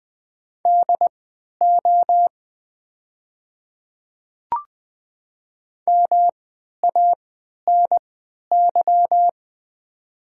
Morse code, heard as a keyed tone, about 21 wpm, spaced as for 11 wpm, 700 Hz. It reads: DO MANY
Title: 3x Character Spacing - Top 300 Words - 20wpm